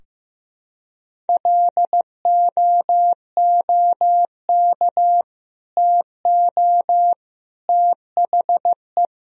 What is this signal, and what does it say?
Morse code, heard as a keyed tone, about 16 wpm, 700 Hz. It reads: LOOK TO THE
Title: Sentences from Top 1000 Words - Review Part 3 - 15wpm